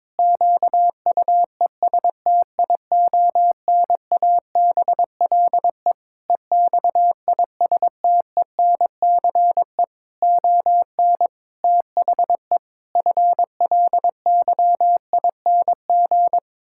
Morse code, heard as a keyed tone, about 22 wpm, 700 Hz. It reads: QUESTIONABLE EXISTENCE ON THE FLYING
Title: Pangram Challenge v2 - 22wpm